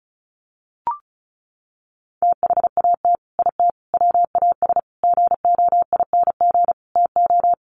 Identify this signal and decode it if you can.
Morse code, heard as a keyed tone, about 34 wpm, 700 Hz. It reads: THAT IT WAS GOING TO
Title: Sets of 5 Words Encore2 35wpm